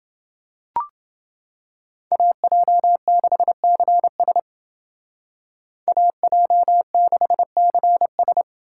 Morse code, heard as a keyed tone, about 29 wpm, 700 Hz. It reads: AJ6CS AJ6CS